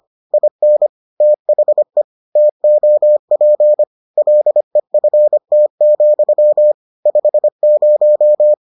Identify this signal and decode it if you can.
Morse code, heard as a keyed tone, about 25 wpm, 600 Hz. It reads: IN THE TOP LEFT, 50